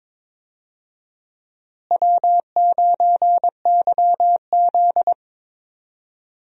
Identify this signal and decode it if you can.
Morse code, heard as a keyed tone, about 22 wpm, 700 Hz. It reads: W9YZ
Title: USA Call Signs - 4 Characters - Repeated 3x - 22wpm